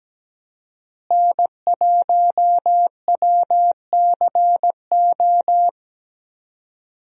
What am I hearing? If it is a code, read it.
Morse code, heard as a keyed tone, about 17 wpm, 700 Hz. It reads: N1WCO
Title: USA Call Signs - 5 Characters - Repeated 3x - 17wpm